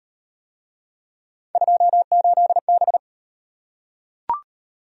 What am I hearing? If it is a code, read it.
Morse code, heard as a keyed tone, about 38 wpm, 700 Hz. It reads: 28B